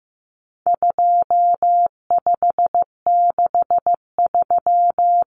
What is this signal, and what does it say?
Morse code, heard as a keyed tone, about 15 wpm, 700 Hz. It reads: 2563
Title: Common Cut Numbers - 4-Digit 15wpm